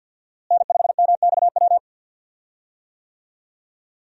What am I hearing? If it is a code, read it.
Morse code, heard as a keyed tone, about 51 wpm, 700 Hz. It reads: N5MXW
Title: USA Call Signs - 5 Characters - Repeated 3x - 50wpm